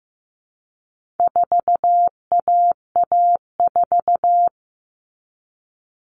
Morse code, heard as a keyed tone, about 15 wpm, 700 Hz. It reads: 4AA4
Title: All Cut Numbers - 4-Digit 15wpm